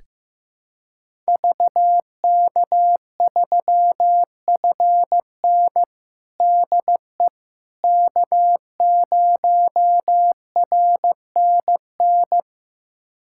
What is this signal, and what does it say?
Morse code, heard as a keyed tone, about 15 wpm, 700 Hz. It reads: VK3FN DE K0RNN
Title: USA Callsign DE Callsign - 5 Characters - 15wpm